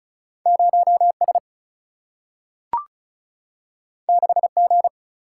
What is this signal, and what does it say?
Morse code, heard as a keyed tone, about 36 wpm, 700 Hz. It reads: 0S 6G
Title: Single-Letter Single-Number 35wpm